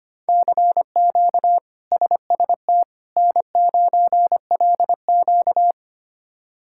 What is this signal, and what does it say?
Morse code, heard as a keyed tone, about 25 wpm, 700 Hz. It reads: CQ SST N9LQ
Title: K1USN SST QSO Practice - 25wpm